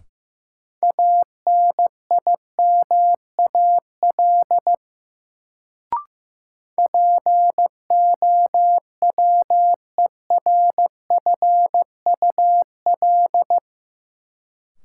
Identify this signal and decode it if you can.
Morse code, heard as a keyed tone, about 15 wpm, 700 Hz. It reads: ANIMAL POWERFUL
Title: Sentences from Top 1300 Words - New Word Drill Alt - 15wpm